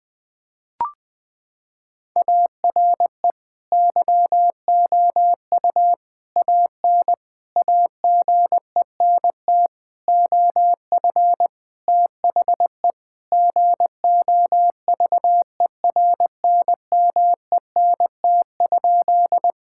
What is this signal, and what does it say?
Morse code, heard as a keyed tone, about 20 wpm, 700 Hz. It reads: ARE YOU AN AGENT OF THE GOVERNMENT?
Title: Sentences from Top 1600 Words 20wpm